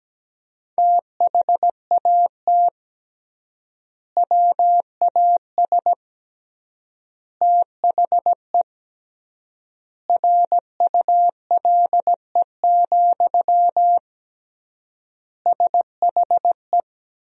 Morse code, heard as a keyed tone, about 17 wpm, 700 Hz. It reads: THAT WAS THE RULE, SHE
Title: 3x Word-Spacing - Sentences from Top 400 Words - 17wpm